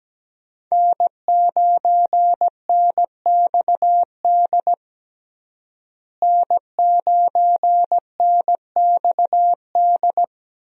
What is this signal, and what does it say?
Morse code, heard as a keyed tone, about 17 wpm, 700 Hz. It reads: N9NXD N9NXD
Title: USA Call Signs - 5 Characters - Repeated 2x - 17wpm